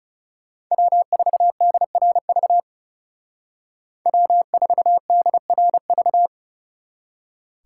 Morse code, heard as a keyed tone, about 39 wpm, 700 Hz. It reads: W4DRV W4DRV